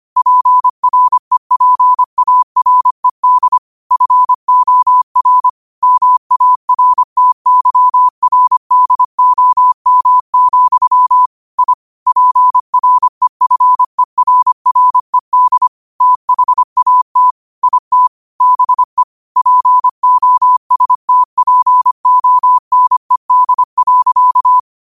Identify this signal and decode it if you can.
Morse code, heard as a keyed tone, about 25 wpm, 1000 Hz. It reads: PREPARED FOR MARTYRDOM, I PREFERRED THAT IT BE POSTPONED.